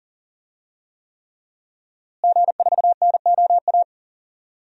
Morse code, heard as a keyed tone, about 40 wpm, 700 Hz. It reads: G4NOA
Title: DXCC-100 Call Signs - 5 Characters - Repeated 3x - 40wpm